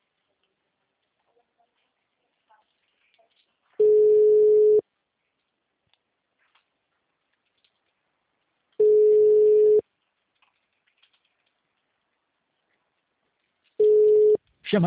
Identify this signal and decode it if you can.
Ringback tone (Europe, 425 Hz).
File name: out- -2005- - - .46866